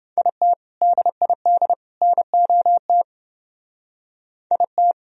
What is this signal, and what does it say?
Morse code, heard as a keyed tone, about 29 wpm, 700 Hz. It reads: IT DID NOT IT